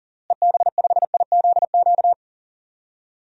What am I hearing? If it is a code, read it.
Morse code, heard as a keyed tone, about 40 wpm, 700 Hz. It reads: EB5IZQ
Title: DXCC-100 Call Signs - 6 Characters - Repeated 3x - 40wpm